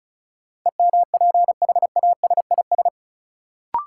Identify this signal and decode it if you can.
Morse code, heard as a keyed tone, about 36 wpm, 700 Hz. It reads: EMPHASIS
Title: Sentences from Top 1400 Words - New Word Drill Alt - 35wpm